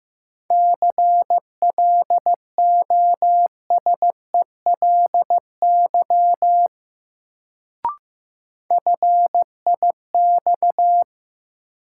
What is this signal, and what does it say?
Morse code, heard as a keyed tone, about 15 wpm, 700 Hz. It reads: CLOSELY FIX
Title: Sentences from Top 1400 Words - New Word Drill Alt - 15wpm